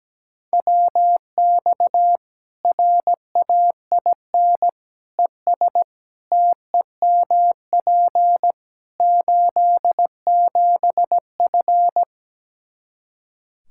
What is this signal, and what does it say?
Morse code, heard as a keyed tone, about 17 wpm, 700 Hz. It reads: WX RAIN ES TEMP 87F
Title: Practice QSO - Exchange 2 - Info - 17wpm